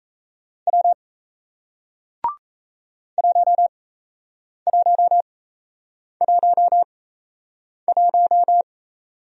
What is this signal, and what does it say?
Morse code, heard as a keyed tone, about 38 wpm, spaced as for 8 wpm, 700 Hz. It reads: W 1111